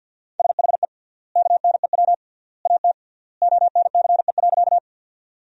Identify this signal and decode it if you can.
Morse code, heard as a keyed tone, about 50 wpm, 700 Hz. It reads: SHE KNEW AT ONCE.
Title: 3x Word-Spacing - Sentences from Top 400 Words - 50wpm